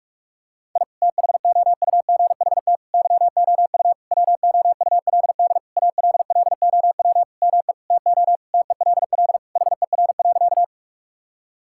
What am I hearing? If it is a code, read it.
Morse code, heard as a keyed tone, about 46 wpm, 700 Hz. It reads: I THOUGHT YOU WOULD ALLOW ME TO TELL HER.
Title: Sentences from Top 1000 Words - Review Part 5 - 45wpm